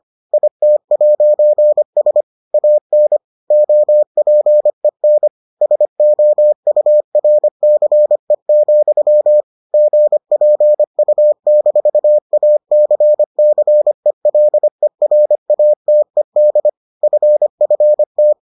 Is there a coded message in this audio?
Yes: Morse code from 0.3 to 18.3 s, heard as a keyed tone, about 25 wpm, 600 Hz. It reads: IT'S AN OPEN SOURCE, GPU-ACCELERATED FFT